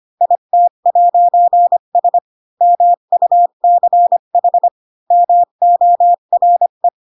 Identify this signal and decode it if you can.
Morse code, heard as a keyed tone, about 25 wpm, 700 Hz. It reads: IT'S MUCH MORE
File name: Pinocchio / Pinocchio ch4 20 25 ik1hge